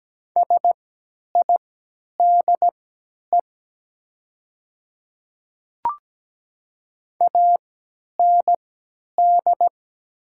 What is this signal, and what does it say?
Morse code, heard as a keyed tone, about 17 wpm, spaced as for 10 wpm, 700 Hz. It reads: SIDE AND
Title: 3x Character Spacing - Top 100 Words - 17wpm